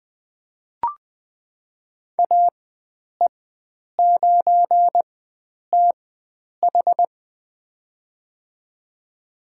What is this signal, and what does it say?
Morse code, heard as a keyed tone, about 20 wpm, spaced as for 9 wpm, 700 Hz. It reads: AE9TH
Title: USA Call Signs - 5 Characters - 4x Character Spacing - 20wpm